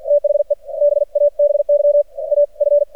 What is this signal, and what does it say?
Morse code, heard as a keyed tone, about 50 wpm, spaced as for 42 wpm, 580 Hz, with radio noise. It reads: THE BABY AF
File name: archived / CRSnet